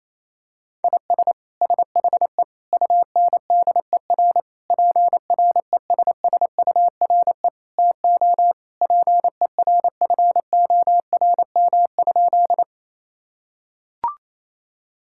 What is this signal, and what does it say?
Morse code, heard as a keyed tone, about 28 wpm, 700 Hz. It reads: IS SHE UNDER PRESSURE TO PERFORM?
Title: Sentences from Top 1000 Words - Review Part 6 - 28wpm